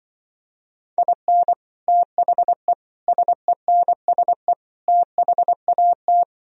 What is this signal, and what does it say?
Morse code, heard as a keyed tone, about 24 wpm, 700 Hz. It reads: IN THE SENSE THAT
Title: Speed-Racing - Sets of 4 Words 20wpm